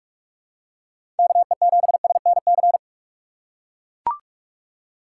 Morse code, heard as a keyed tone, about 43 wpm, 700 Hz. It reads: KE7SNC
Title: USA Call Signs - 6 Characters - 6x Character Spacing - 45wpm